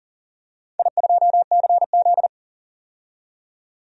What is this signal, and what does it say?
Morse code, heard as a keyed tone, about 41 wpm, 690 Hz. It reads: I2CZ
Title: DXCC-100 Call Signs - 4 Characters - 40wpm